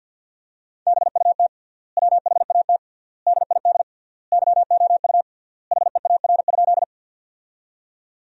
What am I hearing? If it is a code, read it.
Morse code, heard as a keyed tone, about 50 wpm, 700 Hz. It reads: BUT WHAT DID YOU HEAR?
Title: 3x Word-Spacing - Sentences from Top 300 Words - 50wpm